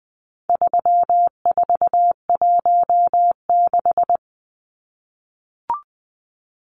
Morse code, heard as a keyed tone, about 20 wpm, 700 Hz. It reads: 3416
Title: Numbers - 4-Digit 20wpm